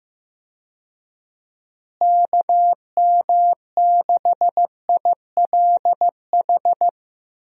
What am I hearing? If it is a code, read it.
Morse code, heard as a keyed tone, about 15 wpm, 700 Hz. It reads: KM6ILH
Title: USA Call Signs - 6 Characters - Repeated 2x - 15wpm